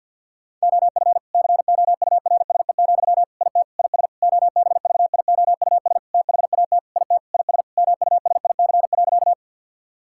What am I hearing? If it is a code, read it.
Morse code, heard as a keyed tone, about 50 wpm, 700 Hz. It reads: OF COURSE, IT IS OBVIOUS THAT IT IS MUSIC.
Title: Sentences from Top 1600 Words 50wpm